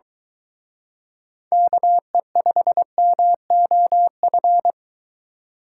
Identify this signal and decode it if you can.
Morse code, heard as a keyed tone, about 22 wpm, 700 Hz. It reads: KE5MOF